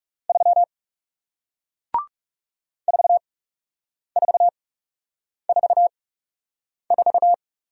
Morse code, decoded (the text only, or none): 3 4444